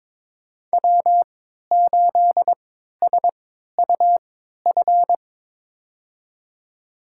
Morse code, heard as a keyed tone, about 22 wpm, spaced as for 13 wpm, 700 Hz. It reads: W8SUF
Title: USA Call Signs - 5 Characters - 3x Character Spacing - 22wpm